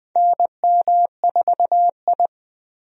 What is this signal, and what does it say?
Morse code, heard as a keyed tone, about 20 wpm, 700 Hz. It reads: NM4I
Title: USA Call Signs - 4 Characters - Repeated 3x - 20wpm